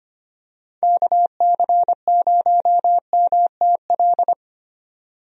Morse code, heard as a keyed tone, about 25 wpm, 700 Hz. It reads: KC0MTL
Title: USA Call Signs - 6 Characters - 25wpm